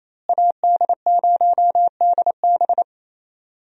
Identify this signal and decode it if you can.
Morse code, heard as a keyed tone, about 28 wpm, 700 Hz. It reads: AD0DB
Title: USA Call Signs - 5 Characters - Repeated 2x - 28wpm